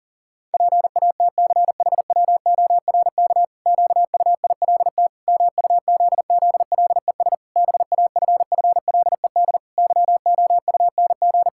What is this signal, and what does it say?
Morse code, heard as a keyed tone, about 40 wpm, 700 Hz. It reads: PATCHWORK QUILT MUZZLES BAFFLED YOUNG
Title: Pangram Challenge v2 - 40wpm